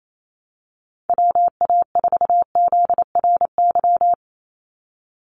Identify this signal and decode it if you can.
Morse code, heard as a keyed tone, about 28 wpm, 700 Hz. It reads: WA4ZRY
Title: USA Call Signs 28wpm